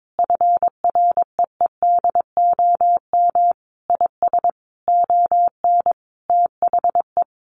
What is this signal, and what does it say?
Morse code, heard as a keyed tone, about 22 wpm, 700 Hz. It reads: FREEDOM IS ON THE
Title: Sentences from Top 1300 Words 22wpm